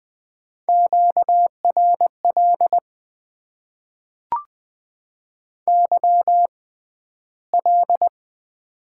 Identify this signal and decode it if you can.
Morse code, heard as a keyed tone, about 20 wpm, spaced as for 12 wpm, 700 Hz. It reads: QRL YL
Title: Top 90 QSO Elements - 6x Character Spacing - 20wpm